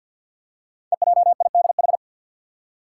Morse code, heard as a keyed tone, about 51 wpm, 690 Hz. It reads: EJIDH
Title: Code Groups - 5 Letters - 50wpm